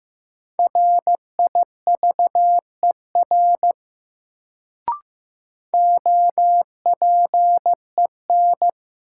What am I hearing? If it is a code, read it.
Morse code, heard as a keyed tone, about 15 wpm, 700 Hz. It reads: RIVER OPEN
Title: Sentences from Top 300 Words - New Word Drill - 15wpm